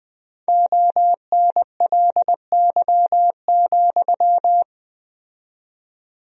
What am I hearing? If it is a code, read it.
Morse code, heard as a keyed tone, about 20 wpm, 700 Hz. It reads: ONLY,
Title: 4x Word-Spacing - Sentences from Top 300 Words - 20wpm